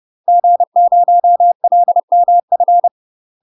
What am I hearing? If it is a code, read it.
Morse code, heard as a keyed tone, about 30 wpm, 700 Hz. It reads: G0LMF
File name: UK MASTER QSOs TRAINING / 30WPM / UK MASTER QSOs 0015